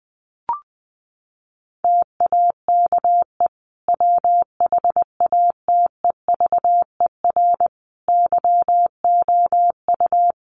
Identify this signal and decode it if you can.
Morse code, heard as a keyed tone, about 20 wpm, 700 Hz. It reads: TAKE WHATEVER YOU
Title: Sentences from Top 1100 Words 20wpm